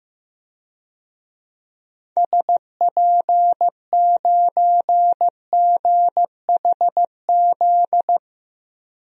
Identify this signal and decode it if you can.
Morse code, heard as a keyed tone, about 15 wpm, 700 Hz. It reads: SP9GHZ
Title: DXCC-100 Call Signs - 6 Characters - Repeated 2x - 15wpm